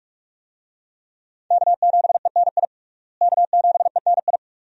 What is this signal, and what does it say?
Morse code, heard as a keyed tone, about 46 wpm, 700 Hz. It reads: K7ENI K7ENI
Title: USA Call Signs - 5 Characters - Repeated 2x - 45wpm